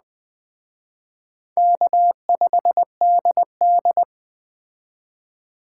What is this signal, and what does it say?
Morse code, heard as a keyed tone, about 20 wpm, 700 Hz. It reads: K5DD
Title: Speed Racing - USA 1x2 Call Signs 17wpm